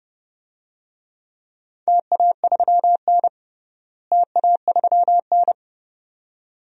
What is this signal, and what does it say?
Morse code, heard as a keyed tone, about 30 wpm, 700 Hz. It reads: TA3N TA3N
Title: DXCC-100 Call Signs - 4 Characters - Repeated 2x - 30wpm